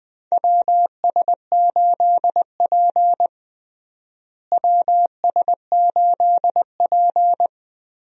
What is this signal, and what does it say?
Morse code, heard as a keyed tone, about 20 wpm, 700 Hz. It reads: WS8P WS8P
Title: USA Call Signs - 4 Characters - Repeated 2x - 20wpm